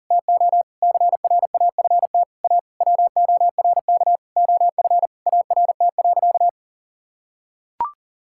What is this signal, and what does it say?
Morse code, heard as a keyed tone, about 37 wpm, 700 Hz. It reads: TO CRAFT A WORK OF ART.
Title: Sentences from Top 1000 Words - Review Part 6 - 40wpm